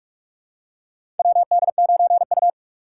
Morse code, heard as a keyed tone, about 44 wpm, 690 Hz. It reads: WD9U
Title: USA Call Signs - 4 Characters - 2x Character Spacing - 45wpm